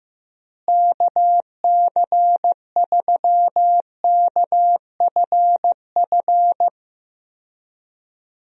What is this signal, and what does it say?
Morse code, heard as a keyed tone, about 15 wpm, 700 Hz. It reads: KC3KFF
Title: USA Call Signs - 6 Characters - Repeated 2x - 15wpm